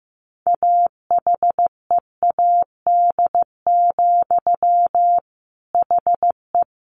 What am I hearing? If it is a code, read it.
Morse code, heard as a keyed tone, about 14 wpm, 700 Hz. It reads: AHEAD, HE
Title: A Sound Of Thunder 15wpm